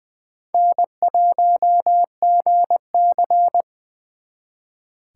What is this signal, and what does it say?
Morse code, heard as a keyed tone, about 24 wpm, 700 Hz. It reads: N1GC
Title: Speed Racing - USA 1x2 Call Signs 20wpm